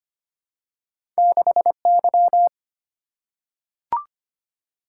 Morse code, heard as a keyed tone, about 25 wpm, 690 Hz. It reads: 6Y
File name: Two Letter-Number Combinations - 25wpm